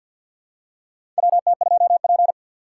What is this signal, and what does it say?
Morse code, heard as a keyed tone, about 50 wpm, 700 Hz. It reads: WT2P
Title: USA Call Signs - 4 Characters - 2x Character Spacing - 50wpm